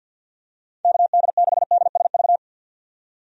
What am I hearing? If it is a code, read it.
Morse code, heard as a keyed tone, about 51 wpm, 700 Hz. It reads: KD6DSV
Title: USA Call Signs - 6 Characters - 50wpm